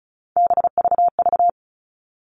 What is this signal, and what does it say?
Morse code, heard as a keyed tone, about 34 wpm, 700 Hz. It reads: BVV
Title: All Cut Numbers - 3-Digit 35wpm